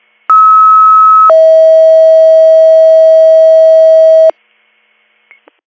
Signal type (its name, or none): two-tone page